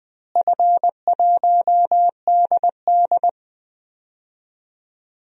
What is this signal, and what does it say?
Morse code, heard as a keyed tone, about 20 wpm, 700 Hz. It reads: F1DD
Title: DXCC-100 Call Signs - 4 Characters - Repeated 2x - 20wpm